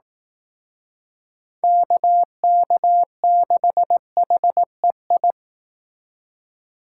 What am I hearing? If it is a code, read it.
Morse code, heard as a keyed tone, about 18 wpm, 700 Hz. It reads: KK6HEI
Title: Speed Racing - USA Call Signs 15wpm